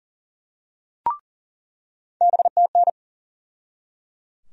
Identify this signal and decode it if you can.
Morse code, heard as a keyed tone, about 37 wpm, 700 Hz. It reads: BTN
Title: USA Call Sign Postfixes - 3 Letters 40wpm